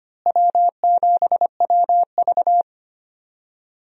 Morse code, heard as a keyed tone, about 25 wpm, 700 Hz. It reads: W7WV
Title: USA Call Signs - 4 Characters - Repeated 3x - 25wpm